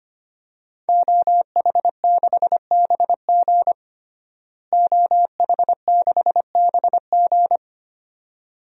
Morse code, heard as a keyed tone, about 25 wpm, 700 Hz. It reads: OH6BG OH6BG
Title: DXCC-100 Call Signs - 5 Characters - Repeated 2x - 25wpm